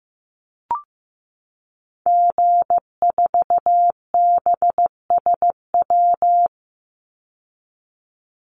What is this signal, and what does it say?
Morse code, heard as a keyed tone, about 15 wpm, 700 Hz. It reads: G4BSW
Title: Call Signs Repeated 2x - 15wpm